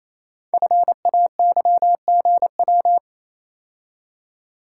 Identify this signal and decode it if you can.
Morse code, heard as a keyed tone, about 27 wpm, 700 Hz. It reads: FAYGW
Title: Code Groups - 5 Letters - 28wpm